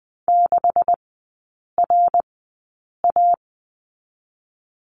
Morse code, heard as a keyed tone, about 20 wpm, spaced as for 8 wpm, 700 Hz. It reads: 6RA
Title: 127 Lesson 21e - Number 6 - Three Characters 20wpm